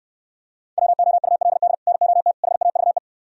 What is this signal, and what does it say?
Morse code, heard as a keyed tone, about 67 wpm, 700 Hz. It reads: WOULD NOT HAVE